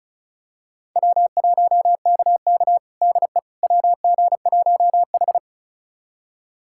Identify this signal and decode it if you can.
Morse code, heard as a keyed tone, about 36 wpm, 700 Hz. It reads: W1KK DE WG1H
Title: USA Callsign DE Callsign - 4 Characters - 35wpm